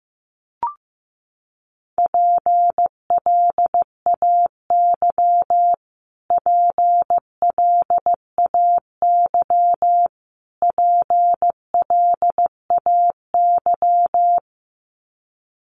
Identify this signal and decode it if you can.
Morse code, heard as a keyed tone, about 15 wpm, 700 Hz. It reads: PLAY PLAY PLAY
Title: Top 200 Words Repeated 15wpm